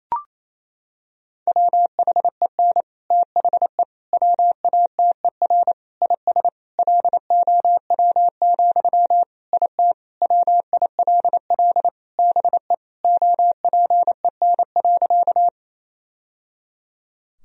Morse code, heard as a keyed tone, about 28 wpm, 700 Hz. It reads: WHEN THE WATER IS LOW, IT WILL BE OPEN.
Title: Sentences from Top 1000 Words - Review Part 1 - 28wpm